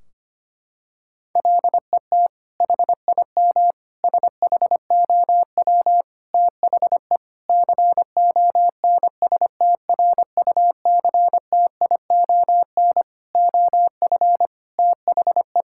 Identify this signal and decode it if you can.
Morse code, heard as a keyed tone, about 25 wpm, 700 Hz. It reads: LET HIM SHOW THE CONSTRUCTION OF THE
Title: Sentences from Top 1600 Words 25wpm